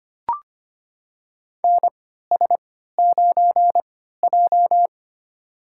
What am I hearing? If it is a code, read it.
Morse code, heard as a keyed tone, about 25 wpm, spaced as for 14 wpm, 700 Hz. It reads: NS9J
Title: USA Call Signs - 4 Characters - 3x Character Spacing - 25wpm